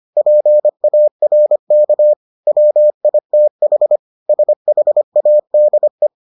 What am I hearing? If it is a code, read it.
Morse code, heard as a keyed tone, about 25 wpm, 600 Hz. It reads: PARK WITH SHADE